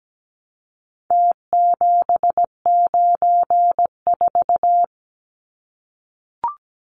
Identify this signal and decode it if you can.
Morse code, heard as a keyed tone, about 17 wpm, 700 Hz. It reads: T794
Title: Numbers - Contest Style - 4-Digit 17wpm